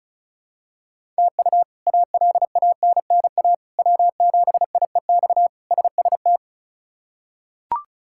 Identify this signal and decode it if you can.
Morse code, heard as a keyed tone, about 34 wpm, 700 Hz. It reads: TU ALANNA W7IEX SST
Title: K1USN SST QSO Practice - 35wpm